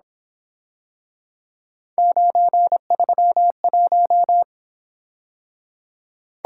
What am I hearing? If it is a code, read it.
Morse code, heard as a keyed tone, about 27 wpm, 700 Hz. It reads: 931
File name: Speed-Racing - Three Letter-Number Combinations - 22wpm